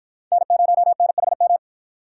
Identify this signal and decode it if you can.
Morse code, heard as a keyed tone, about 48 wpm, 700 Hz. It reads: N0NHM